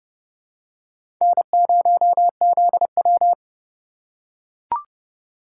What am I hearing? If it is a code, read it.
Morse code, heard as a keyed tone, about 27 wpm, 700 Hz. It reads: N0ZW